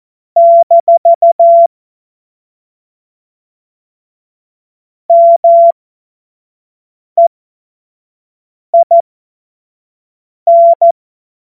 Morse code, heard as a keyed tone, about 14 wpm, spaced as for 5 wpm, 680 Hz. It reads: - MEIN
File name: Morsekurs / Witze / Witz 20 Februar 2024 25 F70